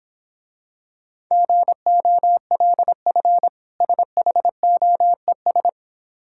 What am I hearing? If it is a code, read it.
Morse code, heard as a keyed tone, about 26 wpm, 700 Hz. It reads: GOLF SHOES